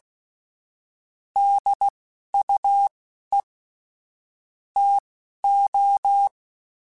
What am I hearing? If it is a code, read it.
Morse code, heard as a keyed tone, about 16 wpm, spaced as for 11 wpm, 800 Hz. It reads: DUE TO
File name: Technical Subjects / Technical Subjects-mp3s / AND Gate @10